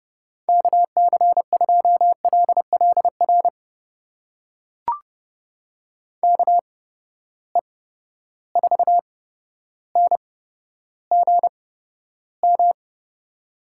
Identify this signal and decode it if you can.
Morse code, heard as a keyed tone, about 30 wpm, spaced as for 13 wpm, 700 Hz. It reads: KC2LLR KE4NGM
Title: USA Call Signs - 6 Characters - 8x Character Spacing - 30wpm